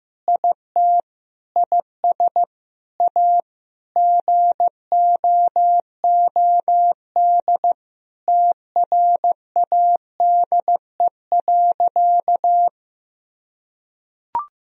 Morse code, heard as a keyed tone, about 15 wpm, 700 Hz. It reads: IT IS A GOOD TRADE.
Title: Sentences from Top 1000 Words - Review Part 4 - 15wpm